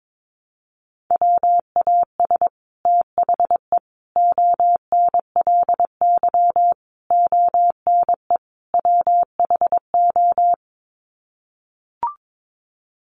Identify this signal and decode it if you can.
Morse code, heard as a keyed tone, about 22 wpm, 700 Hz. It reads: WAS THE ONLY ONE WHO
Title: Sets of 5 Words Encore2 22wpm